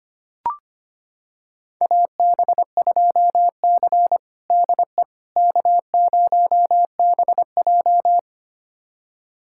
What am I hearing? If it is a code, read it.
Morse code, heard as a keyed tone, about 25 wpm, 700 Hz. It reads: AB2C DE K0BJ
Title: USA Callsign DE Callsign - 4 Characters - 25wpm